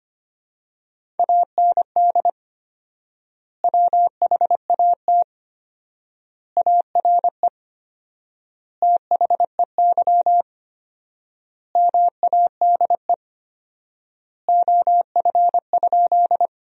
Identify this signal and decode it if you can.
Morse code, heard as a keyed tone, about 25 wpm, 700 Hz. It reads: AND WHAT ARE THEY MADE OF?
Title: 4x Word-Spacing - Sentences from Top 200 Words - 25wpm